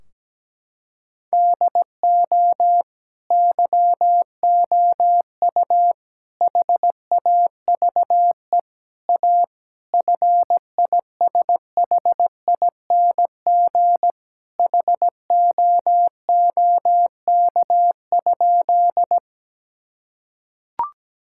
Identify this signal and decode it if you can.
Morse code, heard as a keyed tone, about 17 wpm, 700 Hz. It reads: DO YOU HAVE A FISHING HOOK?
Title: Sentences from Top 1600 Words 17wpm